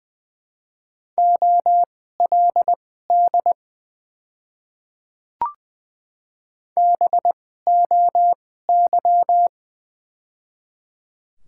Morse code, heard as a keyed tone, about 20 wpm, spaced as for 14 wpm, 700 Hz. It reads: OLD BOY
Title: 2x Character Spacing - Top 200 Words - 20wpm